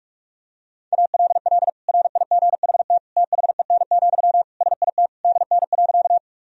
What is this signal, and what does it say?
Morse code, heard as a keyed tone, about 46 wpm, 700 Hz. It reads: ALL RIGHT THEN, SIT DN.
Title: Sentences from Top 500 Words and Abbreviations 45wpm